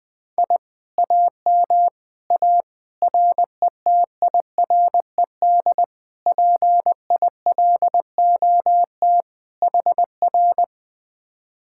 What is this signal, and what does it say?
Morse code, heard as a keyed tone, about 22 wpm, 700 Hz. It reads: I AM A RETIRED PILOT HR
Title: Speed-Racing Common QSO Phrases - 20wpm